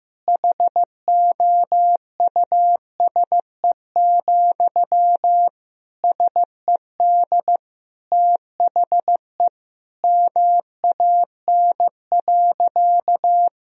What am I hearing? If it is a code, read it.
Morse code, heard as a keyed tone, about 15 wpm, 700 Hz. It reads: HOUSE, SED THE MAN.
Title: Sentences from Top 400 Words and Abbreviations 15wpm